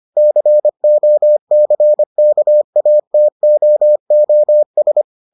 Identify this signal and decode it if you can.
Morse code, heard as a keyed tone, about 25 wpm, 600 Hz. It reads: COCKATOOS